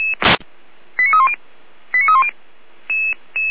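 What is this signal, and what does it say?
Five-tone selective call (ZVEI), heard as quick runs of five tones, tones of about 70 ms. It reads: 89310 89310
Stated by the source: ZVEI five-tone selective call received on an SDR.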